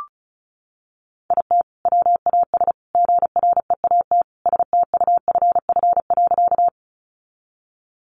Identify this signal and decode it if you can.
Morse code, heard as a keyed tone, about 34 wpm, 700 Hz. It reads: IT WAS GREAT STUFF.
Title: Sentences from Top 1400 Words 35wpm